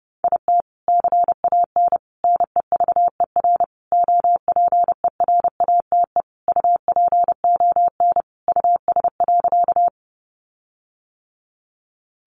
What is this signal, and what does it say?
Morse code, heard as a keyed tone, about 28 wpm, 700 Hz. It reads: IT CAN NEVER OPERATE UPON US.
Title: Sentences from Top 1500 Words 30wpm